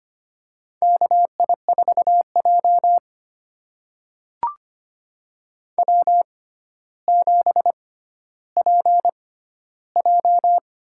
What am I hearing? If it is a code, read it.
Morse code, heard as a keyed tone, about 25 wpm, spaced as for 13 wpm, 700 Hz. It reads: KI4J W7PJ
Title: USA Call Signs - 4 Characters - 6x Character Spacing - 25wpm